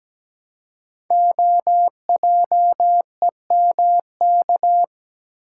Morse code, heard as a keyed tone, about 17 wpm, 700 Hz. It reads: OJEMK
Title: Code Groups - 5 Letters - 17wpm